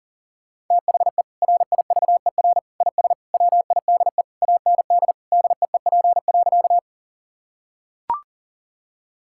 Morse code, heard as a keyed tone, about 37 wpm, 700 Hz. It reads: THE RIVER IS WIDE AND DEEP.
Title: Sentences from Top 1000 Words - Review Part 6 - 40wpm